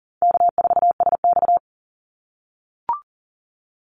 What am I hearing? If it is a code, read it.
Morse code, heard as a keyed tone, about 40 wpm, 700 Hz. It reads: K4SX
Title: Call Signs - 40wpm